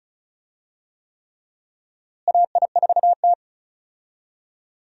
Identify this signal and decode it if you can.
Morse code, heard as a keyed tone, about 36 wpm, 700 Hz. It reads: AI4T
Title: USA Call Signs - 4 Characters - Repeated 3x - 35wpm